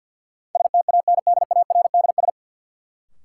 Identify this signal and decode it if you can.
Morse code, heard as a keyed tone, about 53 wpm, 700 Hz. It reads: STANDARDS
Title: Sentences from Top 1000 Words - Review Part 6 - New Word Drill - 50wpm